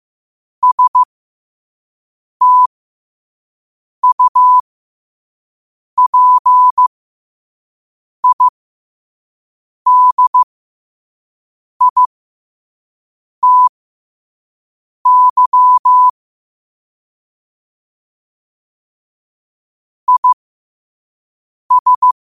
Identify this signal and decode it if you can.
Morse code, heard as a keyed tone, about 15 wpm, spaced as for 5 wpm, 1000 Hz. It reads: STUPIDITY IS
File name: 5wpm